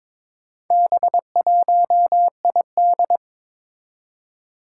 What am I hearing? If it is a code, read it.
Morse code, heard as a keyed tone, about 23 wpm, 700 Hz. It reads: B1ID
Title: Four Challenging Letter-Number Combinations 22wpm